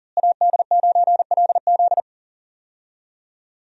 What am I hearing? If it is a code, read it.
Morse code, heard as a keyed tone, about 37 wpm, 700 Hz. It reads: AD9LZ
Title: USA Call Signs - 5 Characters - Repeated 3x - 40wpm